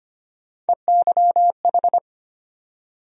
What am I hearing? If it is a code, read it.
Morse code, heard as a keyed tone, about 25 wpm, 700 Hz. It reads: EYH
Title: USA Call Sign Postfixes - 3 Letters 25wpm